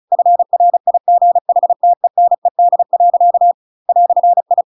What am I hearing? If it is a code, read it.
Morse code, heard as a keyed tone, about 34 wpm, 700 Hz. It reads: FRIGHTENED. "I